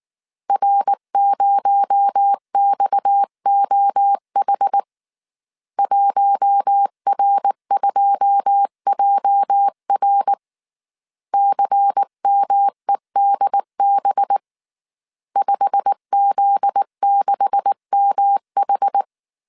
Morse code, heard as a keyed tone, about 19 wpm, 800 Hz. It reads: R0XOH 1R2JR CMEDB 5Z6MH
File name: CW / DET / 1 9 1500C S C 18wpm 2539